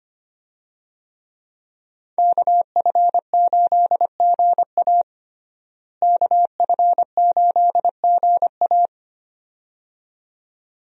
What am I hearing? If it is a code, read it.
Morse code, heard as a keyed tone, about 25 wpm, 700 Hz. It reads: KF8GA KF8GA
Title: USA Call Signs - 5 Characters - Repeated 2x - 25wpm